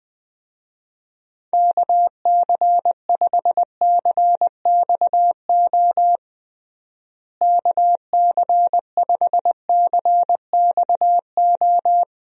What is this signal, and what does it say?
Morse code, heard as a keyed tone, about 21 wpm, 700 Hz. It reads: KC5CXO KC5CXO
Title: USA Call Signs - 6 Characters - Repeated 2x - 20wpm